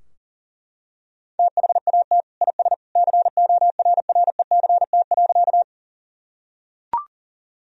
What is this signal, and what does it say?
Morse code, heard as a keyed tone, about 40 wpm, 700 Hz. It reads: THAT IS CORRECT.
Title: Sentences from Top 1000 Words - Review Part 2 - 40wpm